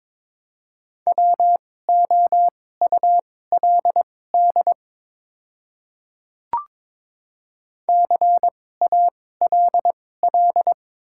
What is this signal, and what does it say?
Morse code, heard as a keyed tone, about 22 wpm, spaced as for 16 wpm, 700 Hz. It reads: WOULD CALL
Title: 2x Character Spacing - Top 100 Words - 22wpm